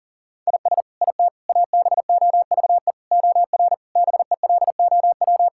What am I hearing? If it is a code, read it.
Morse code, heard as a keyed tone, about 40 wpm, 700 Hz. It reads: IS IT ABOVE OR BELOW
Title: Sentences from Top 1000 Words - Review Part 5 - 40wpm